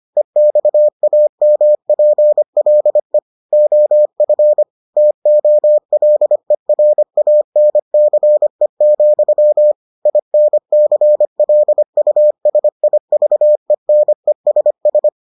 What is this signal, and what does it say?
Morse code, heard as a keyed tone, about 25 wpm, 600 Hz. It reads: EXAMPLE OF TOLERANCE, INCLUSIVENESS